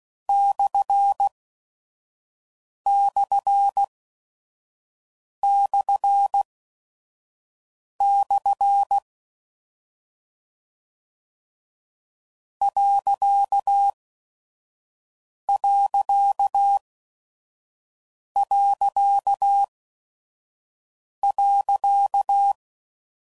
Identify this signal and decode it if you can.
Morse code, heard as a keyed tone, about 16 wpm, spaced as for 5 wpm, 800 Hz. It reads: //// ....
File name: Mixed / Misc / Punct